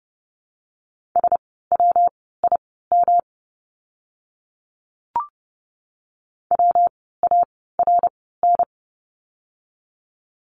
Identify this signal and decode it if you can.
Morse code, heard as a keyed tone, about 30 wpm, spaced as for 17 wpm, 700 Hz. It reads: SWIM WARN
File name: course / 30:10wpm / 115 Lesson 19f - Letter w - Words 30wpm